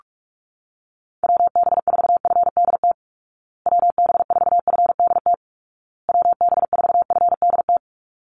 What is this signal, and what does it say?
Morse code, heard as a keyed tone, about 44 wpm, 700 Hz. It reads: WB4FDT WB4FDT WB4FDT
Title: Call Signs Repeated 2x - 45wpm